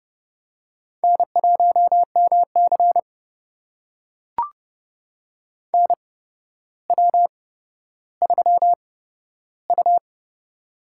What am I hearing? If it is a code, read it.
Morse code, heard as a keyed tone, about 32 wpm, spaced as for 13 wpm, 700 Hz. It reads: N1MC NW3U
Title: USA Call Signs - 4 Characters - 8x Character Spacing - 30wpm